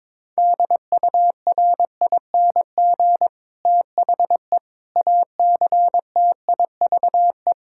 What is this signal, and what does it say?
Morse code, heard as a keyed tone, about 22 wpm, 700 Hz. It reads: DURING THE ACTIVE
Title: Sentences from Top 1000 Words - Review Part 6 - 22wpm